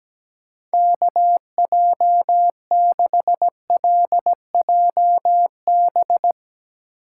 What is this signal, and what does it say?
Morse code, heard as a keyed tone, about 17 wpm, 700 Hz. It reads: KJ6LJB
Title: USA Call Signs - 6 Characters - 8x Character Spacing - 17wpm